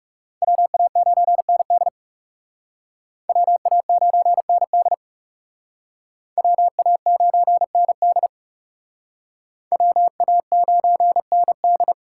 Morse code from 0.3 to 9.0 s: WA9ND WA9ND WA9ND

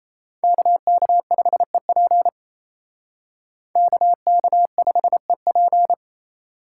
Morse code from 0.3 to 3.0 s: KK5EP

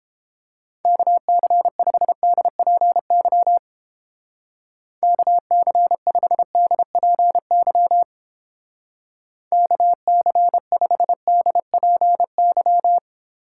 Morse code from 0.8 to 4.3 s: KC5DPY

